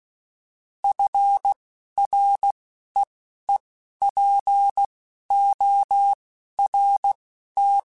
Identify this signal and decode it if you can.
Morse code, heard as a keyed tone, about 16 wpm, spaced as for 11 wpm, 800 Hz. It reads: FREEPORT